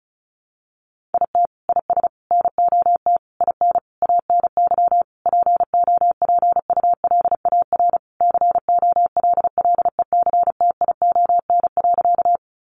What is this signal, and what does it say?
Morse code, heard as a keyed tone, about 34 wpm, 700 Hz. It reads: IT IS NOT IN ANY POPULAR COLLECTION.
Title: Sentences from Top 1200 Words 35wpm